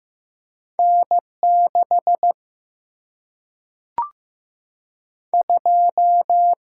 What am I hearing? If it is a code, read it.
Morse code, heard as a keyed tone, about 15 wpm, 700 Hz. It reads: N6 2
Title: Single-Letter Single-Number 15wpm